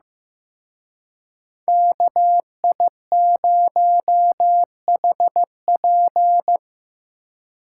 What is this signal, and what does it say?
Morse code, heard as a keyed tone, about 18 wpm, 700 Hz. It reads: KI0HP
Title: Speed Racing - USA 2x2 Call Signs 15wpm